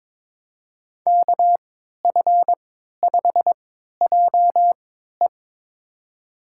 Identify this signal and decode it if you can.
Morse code, heard as a keyed tone, about 22 wpm, spaced as for 12 wpm, 700 Hz. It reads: KF5JE
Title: USA Call Signs - 5 Characters - 3x Character Spacing - 22wpm